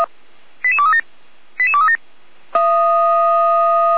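Five-tone selective call (ZVEI), heard as quick runs of five tones, tones of about 70 ms. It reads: 88237 88237
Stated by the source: ZVEI five-tone selective call received with an SDR.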